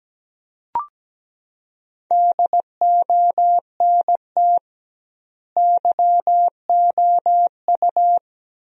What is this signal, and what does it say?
Morse code, heard as a keyed tone, about 17 wpm, 700 Hz. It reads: DONT YOU
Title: 2x Word-Spacing - Sentences from Top 300 Words - 17wpm